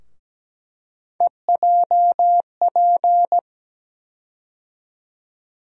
Morse code, heard as a keyed tone, about 17 wpm, 700 Hz. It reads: EJP
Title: USA Call Sign Postfixes - 3 Letters 17wpm